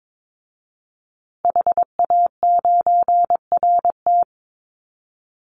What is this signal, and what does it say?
Morse code, heard as a keyed tone, about 22 wpm, 700 Hz. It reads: HA9RT
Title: Call Signs Repeated 2x - 22wpm